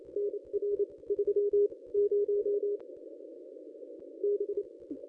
Morse code, heard as a keyed tone, about 29 wpm, spaced as for 22 wpm, 410 Hz. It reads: NR 30 B
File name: contests / ss-cw-k5tr-op-n5rz -1000